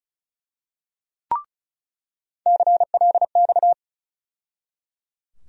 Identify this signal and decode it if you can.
Morse code, heard as a keyed tone, about 34 wpm, 700 Hz. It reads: CLX
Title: USA Call Sign Postfixes - 3 Letters 35wpm